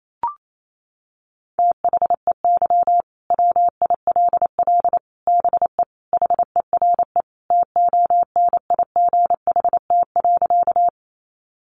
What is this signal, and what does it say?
Morse code, heard as a keyed tone, about 28 wpm, 700 Hz. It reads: THEY WILL BE HERE TONIGHT.
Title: Sentences from Top 600 Words 28wpm